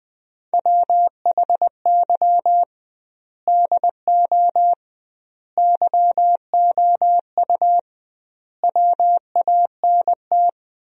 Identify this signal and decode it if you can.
Morse code, heard as a keyed tone, about 20 wpm, 700 Hz. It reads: WHY DO YOU WANT
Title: 2x Word-Spacing - Sentences from Top 200 Words - 20wpm